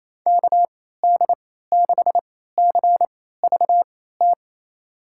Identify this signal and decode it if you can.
Morse code, heard as a keyed tone, about 28 wpm, spaced as for 16 wpm, 700 Hz. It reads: KD6CVT
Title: USA Call Signs - 6 Characters - 3x Character Spacing - 28wpm